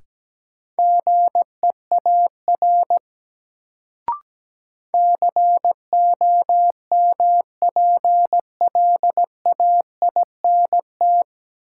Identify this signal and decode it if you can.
Morse code, heard as a keyed tone, about 17 wpm, 700 Hz. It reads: GEAR COMPLAINT
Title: Sentences from Top 1400 Words - New Word Drill - 17wpm